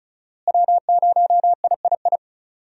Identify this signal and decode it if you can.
Morse code, heard as a keyed tone, about 36 wpm, 700 Hz. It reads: W0III
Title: USA Call Signs - 5 Characters - Repeated 3x - 35wpm